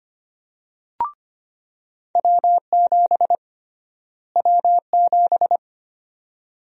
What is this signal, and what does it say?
Morse code, heard as a keyed tone, about 25 wpm, 700 Hz. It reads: W7 W7
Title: USA Call Sign Prefixes - Repeated 2x - 25wpm